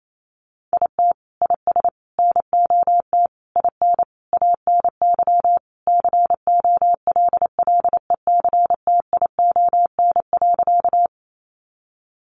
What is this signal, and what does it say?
Morse code, heard as a keyed tone, about 28 wpm, 700 Hz. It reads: IT IS NOT IN ANY COLLECTION.
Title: Sentences from Top 1200 Words 28wpm